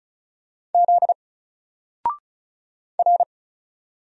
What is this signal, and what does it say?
Morse code, heard as a keyed tone, about 35 wpm, 700 Hz. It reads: Z R
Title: Single Letter-Number Alt 35wpm